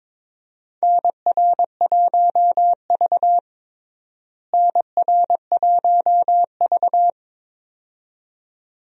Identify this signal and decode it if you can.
Morse code, heard as a keyed tone, about 22 wpm, 700 Hz. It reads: NR1V NR1V
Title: USA Call Signs - 4 Characters - Repeated 2x - 22wpm